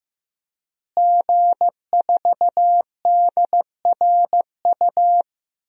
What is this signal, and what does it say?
Morse code, heard as a keyed tone, about 16 wpm, 700 Hz. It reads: G4DRU
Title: DXCC-100 Call Signs - 5 Characters - Repeated 3x - 15wpm